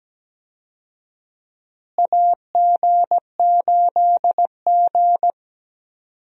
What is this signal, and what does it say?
Morse code, heard as a keyed tone, about 17 wpm, 700 Hz. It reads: AG8G
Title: USA Call Signs - 4 Characters - Repeated 3x - 17wpm